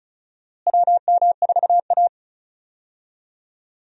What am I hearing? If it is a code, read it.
Morse code, heard as a keyed tone, about 36 wpm, 700 Hz. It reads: WM4A
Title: USA Call Signs - 4 Characters - Repeated 3x - 35wpm